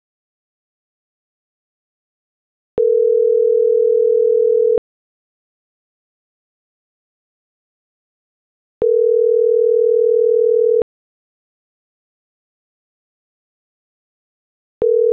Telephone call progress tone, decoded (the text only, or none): ringback tone